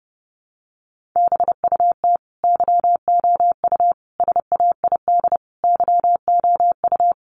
Morse code, heard as a keyed tone, about 30 wpm, 700 Hz. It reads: BUT YOU SAID YOU
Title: Sentences from Top 1400 Words 30wpm